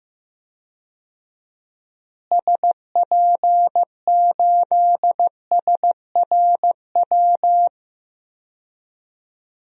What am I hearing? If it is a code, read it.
Morse code, heard as a keyed tone, about 15 wpm, 700 Hz. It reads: SP8SRW
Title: DXCC-100 Call Signs - 6 Characters - Repeated 3x - 15wpm